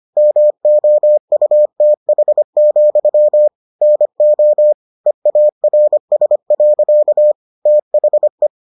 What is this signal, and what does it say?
Morse code, heard as a keyed tone, about 25 wpm, 600 Hz. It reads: MOUTH, NO EARS. THE